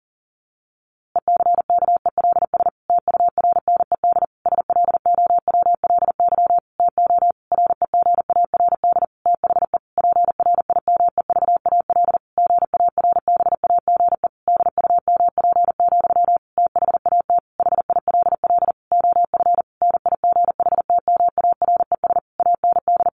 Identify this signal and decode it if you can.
Morse code, heard as a keyed tone, about 40 wpm, 700 Hz. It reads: ECKELS TURNED SLOWLY TO REGARD THE PRIMEVAL GARBAGE DUMP, THAT HILL OF NIGHTMARES AND
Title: A Sound Of Thunder 40wpm